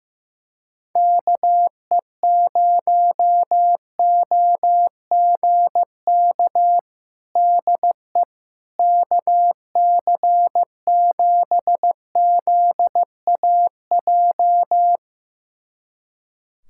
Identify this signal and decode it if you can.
Morse code, heard as a keyed tone, about 16 wpm, 700 Hz. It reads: KE0OGK DE KC7ZAJ
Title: USA Callsign DE Callsign - 6 Characters - 15wpm